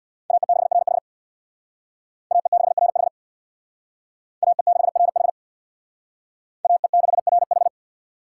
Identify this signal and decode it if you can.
Morse code, heard as a keyed tone, about 61 wpm, 690 Hz. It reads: AE6RH AE6RH AE6RH AE6RH
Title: Speed Racing - USA 2x2 Call Signs 50wpm